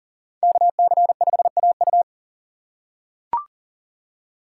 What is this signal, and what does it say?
Morse code, heard as a keyed tone, about 38 wpm, 700 Hz. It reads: KC5AU
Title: USA Call Signs - 5 Characters - 6x Character Spacing - 40wpm